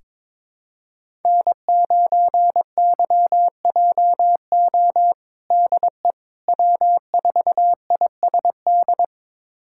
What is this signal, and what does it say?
Morse code, heard as a keyed tone, about 22 wpm, 700 Hz. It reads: N9YJO DE W4ISD
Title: USA Callsign DE Callsign - 5 Characters - 22wpm